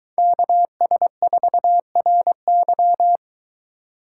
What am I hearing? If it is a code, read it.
Morse code, heard as a keyed tone, about 23 wpm, 700 Hz. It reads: KS4RY